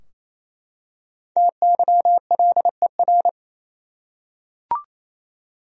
Morse code, heard as a keyed tone, about 28 wpm, 700 Hz. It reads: TYLER
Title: USA Common Names - 28wpm